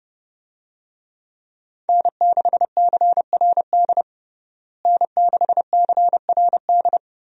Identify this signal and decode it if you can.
Morse code, heard as a keyed tone, about 30 wpm, 700 Hz. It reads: N6CRD N6CRD
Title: USA Call Signs - 5 Characters - Repeated 3x - 30wpm